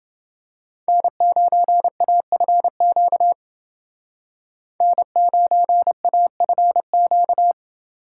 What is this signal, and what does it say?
Morse code, heard as a keyed tone, about 30 wpm, 700 Hz. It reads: N9AFQ N9AFQ